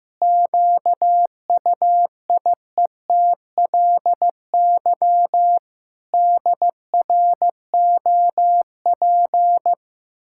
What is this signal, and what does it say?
Morse code, heard as a keyed tone, about 15 wpm, 700 Hz. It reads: QUIETLY DROP
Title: Pangram Challenge v2 - 15wpm